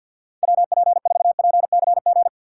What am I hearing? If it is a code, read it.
Morse code, heard as a keyed tone, about 64 wpm, 700 Hz. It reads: WP4PCG